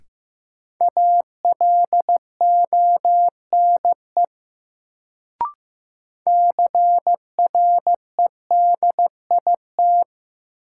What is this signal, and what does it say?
Morse code, heard as a keyed tone, about 15 wpm, 700 Hz. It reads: ALONE CREDIT
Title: Sentences from Top 1000 Words - New Word Drill - 15wpm